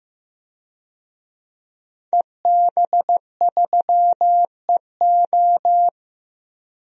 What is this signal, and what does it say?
Morse code, heard as a keyed tone, about 15 wpm, 700 Hz. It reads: EB3EO
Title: DXCC-100 Call Signs - 5 Characters - 15wpm